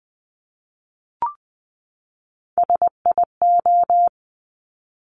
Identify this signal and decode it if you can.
Morse code, heard as a keyed tone, about 20 wpm, 700 Hz. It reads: SIO
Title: Top Three Letter Combinations in Words 20wpm